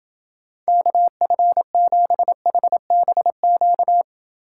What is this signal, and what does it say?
Morse code, heard as a keyed tone, about 30 wpm, 700 Hz. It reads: KF7HBQ